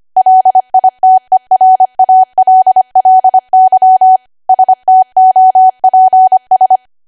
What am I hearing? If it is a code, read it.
Morse code, heard as a keyed tone, about 25 wpm, 750 Hz. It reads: LITERALLY STOPS
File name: Archive / 25 WPM / 25WPM